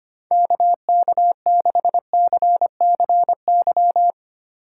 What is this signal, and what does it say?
Morse code, heard as a keyed tone, about 31 wpm, 700 Hz. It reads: KK6CCY
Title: Speed Racing - USA 2x3 Call Signs 25wpm